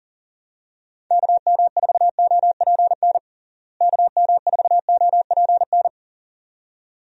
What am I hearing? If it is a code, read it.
Morse code, heard as a keyed tone, about 37 wpm, 700 Hz. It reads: KM4OPN KM4OPN
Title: USA Call Signs - 6 Characters - Repeated 2x - 40wpm